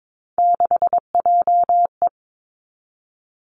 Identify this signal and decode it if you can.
Morse code, heard as a keyed tone, about 22 wpm, 700 Hz. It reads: 6JE
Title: Three Challenging Letter-Number Combinations 22wpm